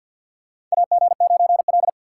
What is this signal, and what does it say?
Morse code, heard as a keyed tone, about 47 wpm, 700 Hz. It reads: AG9L
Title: USA Call Signs - 4 Characters - 6x Character Spacing - 50wpm